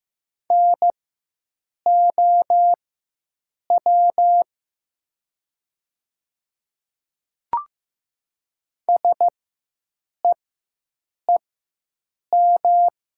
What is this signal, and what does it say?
Morse code, heard as a keyed tone, about 15 wpm, spaced as for 7 wpm, 700 Hz. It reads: NOW SEEM
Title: 4x Character Spacing - Top 500 Words - 15wpm